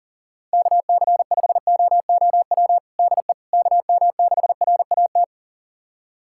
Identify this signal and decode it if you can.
Morse code, heard as a keyed tone, about 37 wpm, 700 Hz. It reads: KC5OOW DE KM6RAT
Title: USA Callsign DE Callsign - 6 Characters - 40wpm